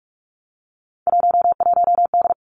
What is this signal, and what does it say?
Morse code, heard as a keyed tone, about 46 wpm, 700 Hz. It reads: 11D
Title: Three Challenging Letter-Number Combinations 45wpm